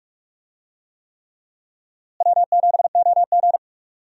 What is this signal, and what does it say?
Morse code, heard as a keyed tone, about 42 wpm, 700 Hz. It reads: W7OG
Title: USA Call Signs - 4 Characters - Repeated 3x - 45wpm